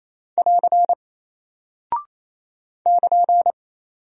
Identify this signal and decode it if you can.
Morse code, heard as a keyed tone, about 28 wpm, 700 Hz. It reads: <AR> <KN>
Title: Single Character (Letter:Number:Punctuation:Prosign) Alt 28wpm